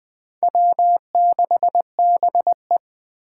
Morse code, heard as a keyed tone, about 20 wpm, 700 Hz. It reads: W6BE